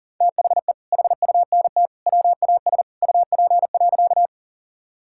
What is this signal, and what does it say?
Morse code, heard as a keyed tone, about 40 wpm, 700 Hz. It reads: THE HUNT WAS UP.
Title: Sentences from Top 1600 Words 40wpm